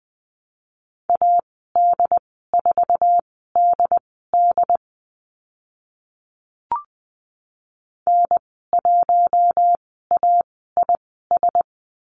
Morse code, heard as a keyed tone, about 20 wpm, spaced as for 14 wpm, 700 Hz. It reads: AD4DD N1AIS
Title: 069 Lesson 12g - Letter d - Call Signs 20wpm